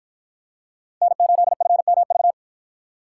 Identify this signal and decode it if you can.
Morse code, heard as a keyed tone, about 53 wpm, 700 Hz. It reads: N8FMV